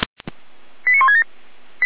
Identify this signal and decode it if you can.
Five-tone selective call (ZVEI), heard as quick runs of five tones, tones of about 70 ms. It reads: 89167
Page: /455/SDR/ZVEI/89163/